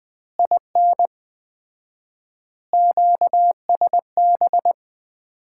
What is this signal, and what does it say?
Morse code, heard as a keyed tone, about 20 wpm, 700 Hz. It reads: IN QSB
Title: Common QSO Phrases - 4x Word-Spacing - 20wpm